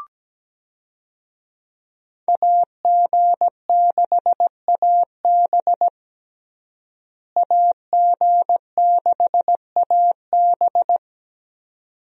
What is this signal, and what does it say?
Morse code, heard as a keyed tone, about 17 wpm, 700 Hz. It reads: AG6AB AG6AB
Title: USA Call Signs - 5 Characters - Repeated 2x - 17wpm